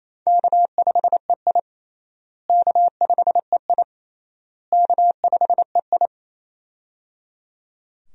Morse code, heard as a keyed tone, about 27 wpm, 700 Hz. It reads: K5EI K5EI K5EI
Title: USA Call Signs - 4 Characters - Repeated 3x - 28wpm